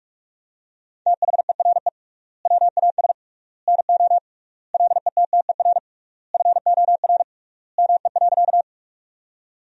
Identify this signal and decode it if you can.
Morse code, heard as a keyed tone, about 46 wpm, 700 Hz. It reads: THERE WAS NO LETTER FOR ME.
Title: 3x Word-Spacing - Sentences from Top 300 Words - 45wpm